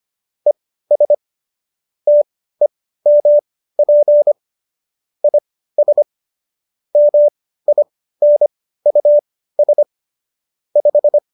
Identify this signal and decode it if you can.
Morse code, heard as a keyed tone, about 25 wpm, spaced as for 15 wpm, 590 Hz. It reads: ES TEMP IS MINUS 5